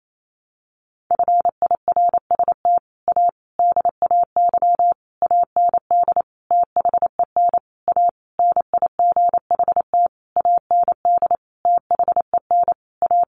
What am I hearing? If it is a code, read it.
Morse code, heard as a keyed tone, about 28 wpm, 700 Hz. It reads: FIRST A DAY AND THEN A NIGHT AND THEN A
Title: A Sound Of Thunder 28wpm